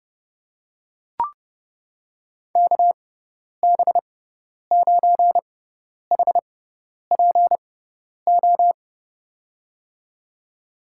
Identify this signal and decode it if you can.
Morse code, heard as a keyed tone, about 32 wpm, spaced as for 10 wpm, 700 Hz. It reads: KB9HPO
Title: USA Call Signs - 6 Characters - 6x Character Spacing - 30wpm